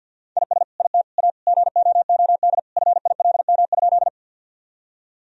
Morse code, heard as a keyed tone, about 50 wpm, 700 Hz. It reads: IS IT A GOOD FILM?
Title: Sentences from Top 1000 Words - Review Part 3 - 50wpm